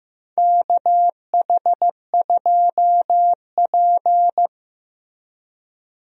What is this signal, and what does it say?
Morse code, heard as a keyed tone, about 15 wpm, 700 Hz. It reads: KH2P
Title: USA Call Signs - 4 Characters - Repeated 2x - 15wpm